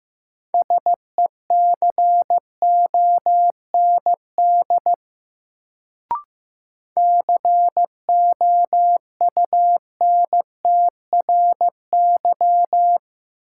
Morse code, heard as a keyed tone, about 15 wpm, 700 Hz. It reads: SECOND COUNTRY
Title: Sentences from Top 300 Words - New Word Drill - 15wpm